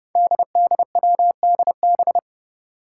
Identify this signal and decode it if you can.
Morse code, heard as a keyed tone, about 30 wpm, 700 Hz. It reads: DDWDB
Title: Five Challenging Letter-Number Combinations 30wpm